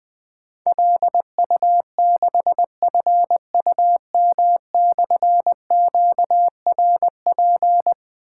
Morse code, heard as a keyed tone, about 21 wpm, 700 Hz. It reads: LU6FUM/QRP
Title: Difficult Call Signs 20wpm